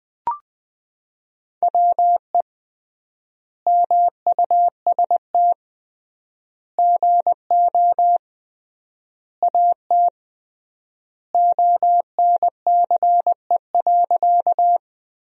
Morse code, heard as a keyed tone, about 20 wpm, 700 Hz. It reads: WE MUST GO AT ONCE.
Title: 3x Word-Spacing - Sentences from Top 300 Words - 20wpm